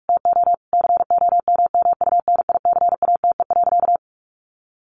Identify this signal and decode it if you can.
Morse code, heard as a keyed tone, about 44 wpm, 700 Hz. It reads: TO COMMUNICATE.
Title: Sentences from Top 1300 Words 45wpm